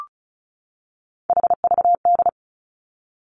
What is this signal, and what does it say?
Morse code, heard as a keyed tone, about 36 wpm, 700 Hz. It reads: HVD
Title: Three Challenging Letter-Number Combinations 35wpm